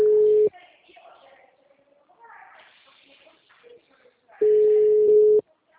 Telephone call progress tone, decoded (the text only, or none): ringback tone